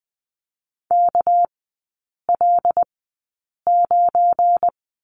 Morse code, heard as a keyed tone, about 20 wpm, spaced as for 8 wpm, 690 Hz. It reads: KL9
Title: 212 Lesson 34e - Number 8 - Three Characters 20wpm